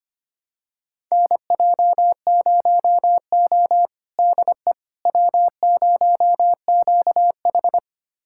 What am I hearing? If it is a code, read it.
Morse code, heard as a keyed tone, about 25 wpm, 700 Hz. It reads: NJ0O DE W0QH
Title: USA Callsign DE Callsign - 4 Characters - 25wpm